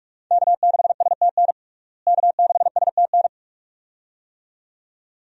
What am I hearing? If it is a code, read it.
Morse code, heard as a keyed tone, about 46 wpm, 700 Hz. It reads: K6STN K6STN
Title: USA Call Signs - 5 Characters - Repeated 2x - 45wpm